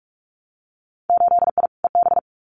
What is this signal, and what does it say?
Morse code, heard as a keyed tone, about 46 wpm, 700 Hz. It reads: 8I EB
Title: Winter Field Day Exchanges Encore 45wpm